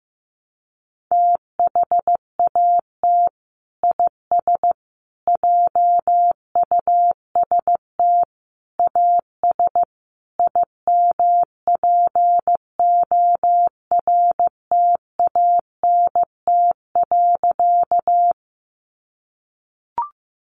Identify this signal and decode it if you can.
Morse code, heard as a keyed tone, about 15 wpm, 700 Hz. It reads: THAT IS JUST AS IMPORTANT.
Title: Sentences from Top 600 Words 15wpm